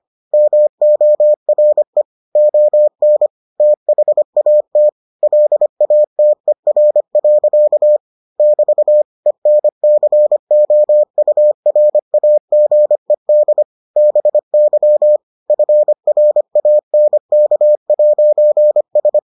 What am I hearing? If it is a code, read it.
Morse code, heard as a keyed tone, about 25 wpm, 600 Hz. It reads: MORE ON THAT LATER. <BT> ENCOURAGED BY FRANK'S